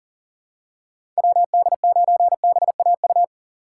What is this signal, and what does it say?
Morse code, heard as a keyed tone, about 40 wpm, 700 Hz. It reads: WD9BAU
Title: USA Call Signs - 6 Characters - 8x Character Spacing - 40wpm